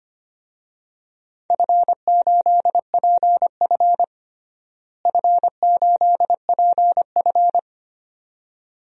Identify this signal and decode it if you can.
Morse code, heard as a keyed tone, about 25 wpm, 700 Hz. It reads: F8PF F8PF
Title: DXCC-100 Call Signs - 4 Characters - Repeated 2x - 25wpm